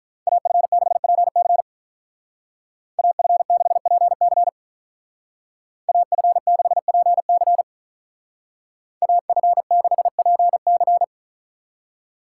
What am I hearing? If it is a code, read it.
Morse code, heard as a keyed tone, about 43 wpm, 700 Hz. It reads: AF6PC AF6PC AF6PC AF6PC
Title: Speed Racing - USA 2x2 Call Signs 35wpm